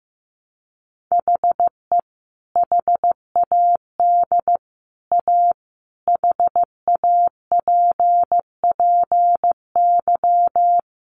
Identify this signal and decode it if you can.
Morse code, heard as a keyed tone, about 15 wpm, 700 Hz. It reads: HE HAD A HAPPY
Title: Sentences from Top 1000 Words 15wpm